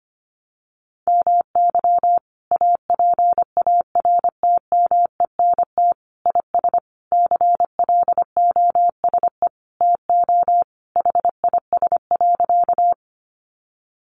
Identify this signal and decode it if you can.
Morse code, heard as a keyed tone, about 25 wpm, 700 Hz. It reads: MY APARTMENT IS CLOSE TO HIS.
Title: Sentences from Top 900 Words 25wpm